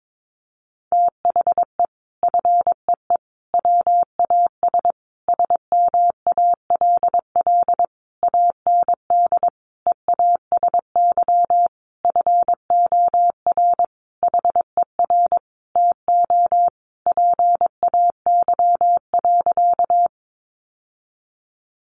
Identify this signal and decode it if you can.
Morse code, heard as a keyed tone, about 22 wpm, 700 Hz. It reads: THE FEE WAS SMALL AND EASY FOR HER TO PAY.
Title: Sentences from Top 1500 Words 22wpm